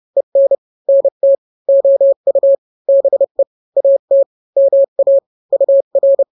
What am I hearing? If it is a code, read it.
Morse code, heard as a keyed tone, about 30 wpm, 550 Hz. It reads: EN NT OU BE AT MA UR
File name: n-grams / 2-grams 30 25 WPM